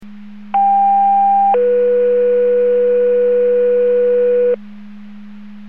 Two-tone page, heard as a one-second tone, then a three-second tone, about 800 then 490 Hz, with noise. It is heard from 0.5 to 4.6 s.
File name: Portage 154.385 Paging